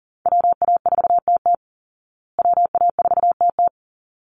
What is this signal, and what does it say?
Morse code, heard as a keyed tone, about 40 wpm, 700 Hz. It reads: WA4TT WA4TT
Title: Call Signs Repeated 3x - 40wpm